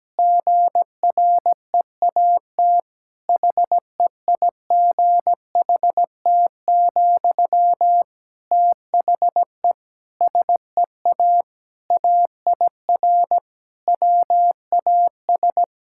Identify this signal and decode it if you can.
Morse code, heard as a keyed tone, about 17 wpm, 700 Hz. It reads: GREAT HEIGHT, THE SEA AIR WAS
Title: Sentences from Top 1600 Words 17wpm